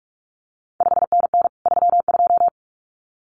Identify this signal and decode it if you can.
Morse code, heard as a keyed tone, about 46 wpm, 700 Hz. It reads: 5NN 32
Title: CQ WW DX Contest Exchange Encore - 45wpm